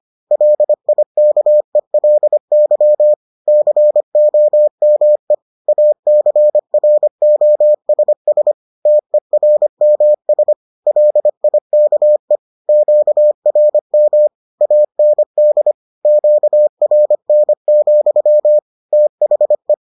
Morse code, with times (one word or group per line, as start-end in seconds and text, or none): 0.3-3.1 LIKELY
3.5-5.3 COME
5.7-8.5 ACROSS
8.9-10.5 TERMS
10.9-12.4 LIKE
12.7-14.3 QRM
14.6-15.7 AND
16.1-18.6 QRN,
18.9-19.7 THE